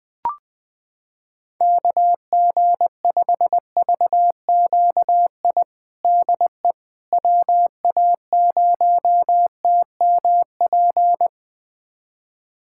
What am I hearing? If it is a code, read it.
Morse code, heard as a keyed tone, about 21 wpm, 700 Hz. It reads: KG5VQI DE WA0TMP
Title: USA Callsign DE Callsign 20wpm